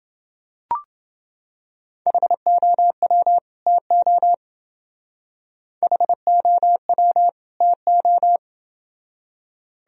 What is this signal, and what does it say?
Morse code, heard as a keyed tone, about 28 wpm, 700 Hz. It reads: HOW TO HOW TO